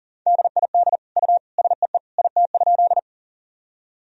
Morse code, heard as a keyed tone, about 40 wpm, 700 Hz. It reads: DID U SEE IT?
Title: Sentences from Top 100 Words and Abbreviations 40wpm